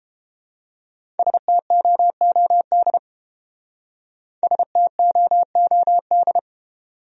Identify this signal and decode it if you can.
Morse code, heard as a keyed tone, about 30 wpm, 700 Hz. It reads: STOOD STOOD